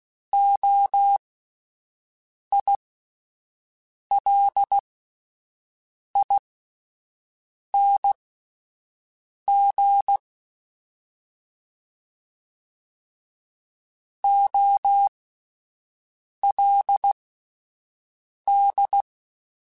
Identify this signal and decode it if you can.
Morse code, heard as a keyed tone, about 16 wpm, spaced as for 5 wpm, 800 Hz. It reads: OILING OLD
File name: Alphabetical / Alphabetical mp3s / Os @5